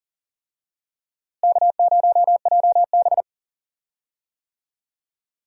Morse code, heard as a keyed tone, about 40 wpm, 700 Hz. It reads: K0JB
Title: USA Call Signs - 4 Characters - Repeated 3x - 40wpm